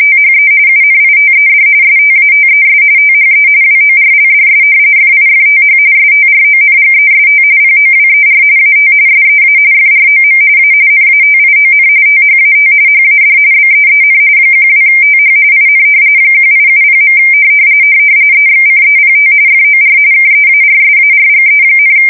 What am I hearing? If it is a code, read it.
RTTY, about 45.45 baud. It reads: THE UNITED STATES THE WORLDS FIFTH STRONGEST POWER AT SEA FROM A NATION THAT HAD BEEN 12TH IN 1870. EXCEPT FOR KEARSARGE,
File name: fsk170 / battleships